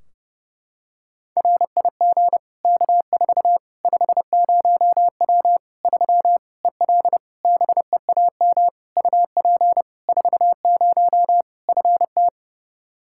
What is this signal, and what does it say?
Morse code, heard as a keyed tone, about 32 wpm, 700 Hz. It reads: RIG K4 50W 3 EL BEAM UP 40 FT
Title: Practice QSO - Exchange 2 - Info - 30wpm